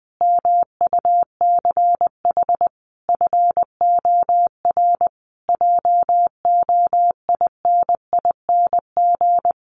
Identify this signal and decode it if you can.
Morse code, heard as a keyed tone, about 20 wpm, 700 Hz. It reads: MUCH FOR JOINING
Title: Sets of 5 Words Encore2 20wpm